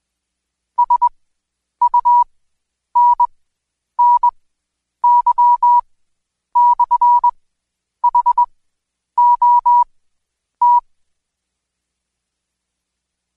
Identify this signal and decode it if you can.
Morse code, heard as a keyed tone, about 21 wpm, spaced as for 9 wpm, 970 Hz. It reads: SUNNY/HOT
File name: Code Quick / AudioDownld2 / Seg116-Day21-DayDrill 21